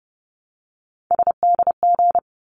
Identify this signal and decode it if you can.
Morse code, heard as a keyed tone, about 30 wpm, 700 Hz. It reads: SDG
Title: ARRL Sections 30wpm